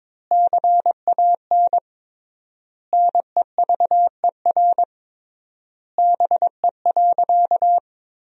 Morse code, heard as a keyed tone, about 22 wpm, 700 Hz. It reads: CAN NEVER BE.
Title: 3x Word-Spacing - Sentences from Top 300 Words - 22wpm